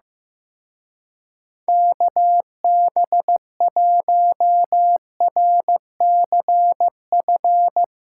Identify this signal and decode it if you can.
Morse code, heard as a keyed tone, about 16 wpm, 700 Hz. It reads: KB1RCF
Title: Speed Racing - USA 2x3 Call Signs 15wpm